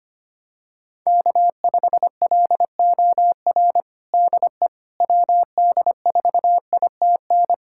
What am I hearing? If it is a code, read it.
Morse code, heard as a keyed tone, about 25 wpm, 700 Hz. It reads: K5LOR DE WD4ITN
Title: USA Callsign DE Callsign 25wpm